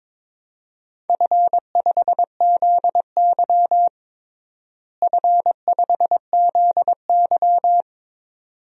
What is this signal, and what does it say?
Morse code, heard as a keyed tone, about 22 wpm, 700 Hz. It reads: F5ZY F5ZY
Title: DXCC-100 Call Signs - 4 Characters - Repeated 2x - 22wpm